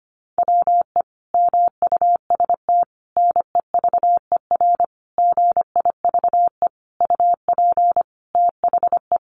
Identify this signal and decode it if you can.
Morse code, heard as a keyed tone, about 25 wpm, 700 Hz. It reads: WE MUST NEVER GIVE UP THE
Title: Sentences from Top 1500 Words 25wpm